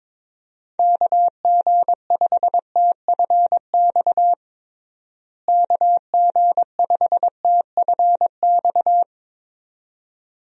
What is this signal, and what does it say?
Morse code, heard as a keyed tone, about 22 wpm, 700 Hz. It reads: KG5TFX KG5TFX
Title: USA Call Signs - 6 Characters - Repeated 2x - 22wpm